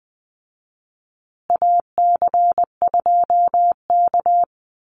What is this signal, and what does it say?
Morse code, heard as a keyed tone, about 19 wpm, 700 Hz. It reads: AC2K
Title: Call Signs Repeated 2x - 20wpm